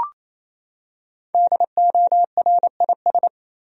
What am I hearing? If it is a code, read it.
Morse code, heard as a keyed tone, about 28 wpm, 700 Hz. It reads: DORIS
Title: USA Common Names - 28wpm